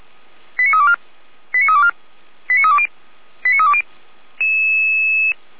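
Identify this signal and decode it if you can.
Five-tone selective call (ZVEI), heard as quick runs of five tones, tones of about 70 ms. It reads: 89324 89324 89320 89320
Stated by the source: ZVEI five-tone selective call received via SDR.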